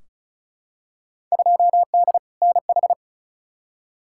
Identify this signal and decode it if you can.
Morse code, heard as a keyed tone, about 36 wpm, 700 Hz. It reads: 2D NH
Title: Field Day Exchanges Encore - 35wpm